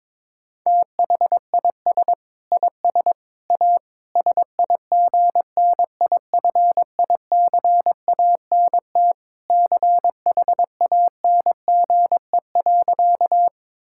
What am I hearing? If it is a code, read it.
Morse code, heard as a keyed tone, about 22 wpm, 700 Hz. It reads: THIS IS A SIGNIFICANT CHANGE.
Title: Sentences from Top 1000 Words - Review Part 6 - 22wpm